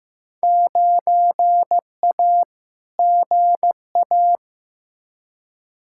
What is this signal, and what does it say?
Morse code, heard as a keyed tone, about 15 wpm, 700 Hz. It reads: 9A GA
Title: Field Day Exchanges Encore - 15wpm